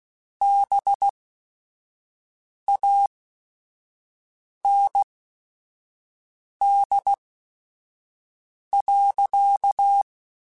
Morse code, heard as a keyed tone, about 16 wpm, spaced as for 5 wpm, 800 Hz. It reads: BAND.